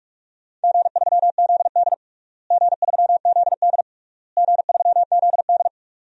Morse code, heard as a keyed tone, about 46 wpm, 700 Hz. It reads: G3ZD G3ZD G3ZD
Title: DXCC-100 Call Signs - 4 Characters - Repeated 3x - 45wpm